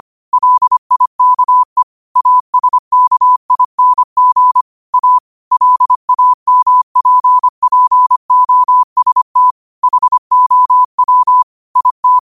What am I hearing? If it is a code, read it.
Morse code, heard as a keyed tone, about 25 wpm, 1000 Hz. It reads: LIKE ASKING A LAMPPOST HOW IT
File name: qotdfiles / 25wpm